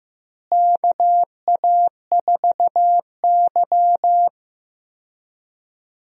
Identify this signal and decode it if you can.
Morse code, heard as a keyed tone, about 16 wpm, 700 Hz. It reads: KA4Y
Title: USA Call Signs - 4 Characters - Repeated 2x - 15wpm